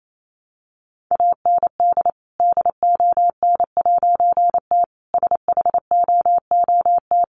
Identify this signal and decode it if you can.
Morse code, heard as a keyed tone, about 28 wpm, 700 Hz. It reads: AND DON'T SHOOT
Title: A Sound Of Thunder 28wpm